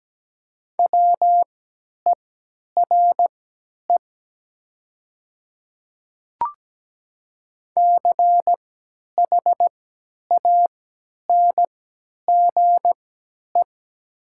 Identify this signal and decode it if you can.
Morse code, heard as a keyed tone, about 17 wpm, spaced as for 10 wpm, 700 Hz. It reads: WERE CHANGE
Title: 3x Character Spacing - Top 500 Words - 17wpm